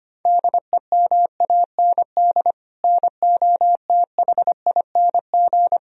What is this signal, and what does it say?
Morse code, heard as a keyed tone, about 25 wpm, 700 Hz. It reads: DEMAND NOTHING
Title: Sentences from Top 1000 Words - Review Part 4 - 25wpm